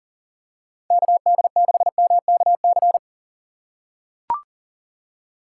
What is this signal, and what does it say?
Morse code, heard as a keyed tone, about 40 wpm, 700 Hz. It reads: KD6MKC
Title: USA Call Signs - 6 Characters - 3x Character Spacing - 40wpm